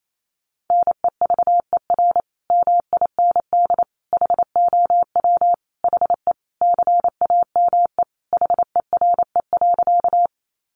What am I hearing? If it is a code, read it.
Morse code, heard as a keyed tone, about 28 wpm, 700 Hz. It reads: NEVER MIND HOW HE CAME HERE.
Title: Sentences from Top 500 Words 28wpm